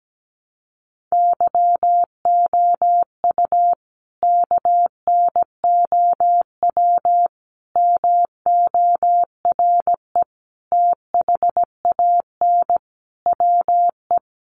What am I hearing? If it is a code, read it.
Morse code, heard as a keyed tone, about 17 wpm, 700 Hz. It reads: YOU KNOW MORE THAN WE
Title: Sentences from Top 100 Words 17wpm